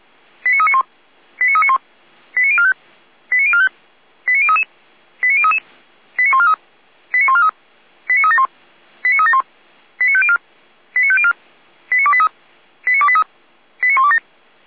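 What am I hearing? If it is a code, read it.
Five-tone selective call (ZVEI), heard as quick runs of five tones, tones of about 70 ms. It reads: 89391 89391 89045 89045 89033 89033 89143 89143 89371 89371 89504 89504 89283 89283 89127
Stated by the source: ZVEI five-tone selective call received on SDR.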